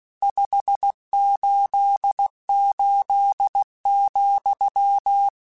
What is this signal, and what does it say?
Morse code, heard as a keyed tone, about 16 wpm, 800 Hz. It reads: 588,